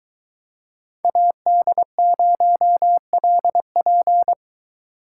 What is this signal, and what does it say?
Morse code, heard as a keyed tone, about 23 wpm, 700 Hz. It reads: AD0LP